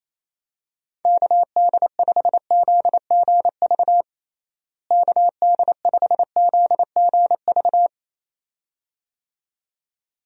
Morse code, heard as a keyed tone, about 28 wpm, 700 Hz. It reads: KD5ZGV KD5ZGV
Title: USA Call Signs - 6 Characters - Repeated 2x - 28wpm